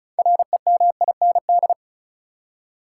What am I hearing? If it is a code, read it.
Morse code, heard as a keyed tone, about 34 wpm, 700 Hz. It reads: REMIND
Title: Six Letter Words - 35wpm